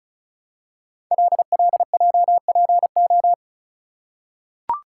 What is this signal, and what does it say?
Morse code, heard as a keyed tone, about 35 wpm, 690 Hz. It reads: LLJPO
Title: Code Groups - 5 Letters - 35wpm